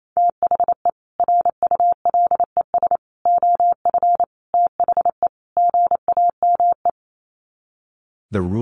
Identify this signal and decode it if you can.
Morse code, heard as a keyed tone, about 28 wpm, 700 Hz. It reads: THE RULES OF THE GAME
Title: Sets of 5 Words Encore2 28wpm